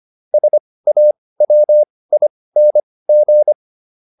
Morse code, heard as a keyed tone, about 25 wpm, spaced as for 18 wpm, 600 Hz. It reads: SAWING
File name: ING6 18